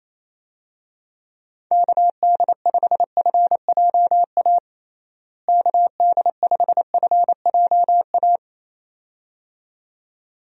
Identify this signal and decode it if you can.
Morse code, heard as a keyed tone, about 27 wpm, 700 Hz. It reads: KD5FJA KD5FJA
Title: USA Call Signs - 6 Characters - Repeated 2x - 28wpm